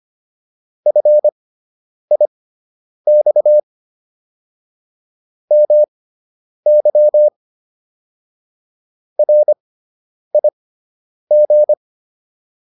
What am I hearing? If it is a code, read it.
Morse code, heard as a keyed tone, about 25 wpm, spaced as for 9 wpm, 600 Hz. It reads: FIX MY RIG